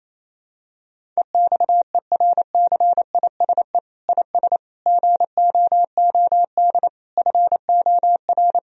Morse code, heard as a keyed tone, about 28 wpm, 700 Hz. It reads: EXERCISE IS GOOD FOR
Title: Sentences from Top 1000 Words - Review Part 4 - 28wpm